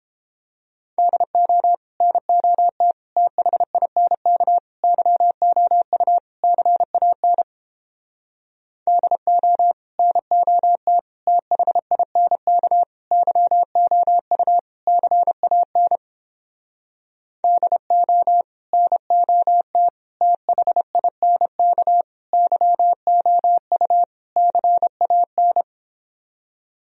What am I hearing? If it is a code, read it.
Morse code, heard as a keyed tone, about 29 wpm, 700 Hz. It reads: DO NOT THINK YOU CAN DO NOT THINK YOU CAN DO NOT THINK YOU CAN